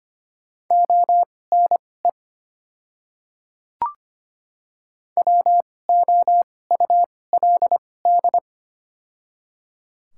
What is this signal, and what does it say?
Morse code, heard as a keyed tone, about 25 wpm, spaced as for 18 wpm, 690 Hz. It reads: ONE WOULD
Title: 2x Character Spacing - Top 100 Words - 25wpm